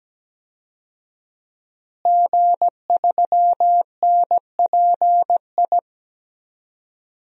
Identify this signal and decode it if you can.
Morse code, heard as a keyed tone, about 17 wpm, 700 Hz. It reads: G3NPI
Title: DXCC-100 Call Signs - 5 Characters - Repeated 2x - 17wpm